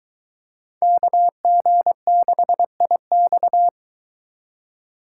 Morse code, heard as a keyed tone, about 23 wpm, 700 Hz. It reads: KG6IX